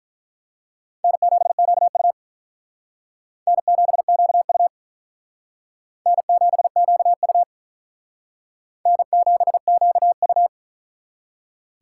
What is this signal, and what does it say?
Morse code, heard as a keyed tone, about 45 wpm, 700 Hz. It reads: N7QU N7QU N7QU N7QU